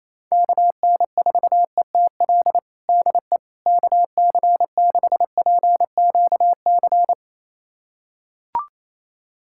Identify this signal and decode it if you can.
Morse code, heard as a keyed tone, about 28 wpm, 700 Hz. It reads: KN4ETL DE KC6PQC
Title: USA Callsign DE Callsign 28wpm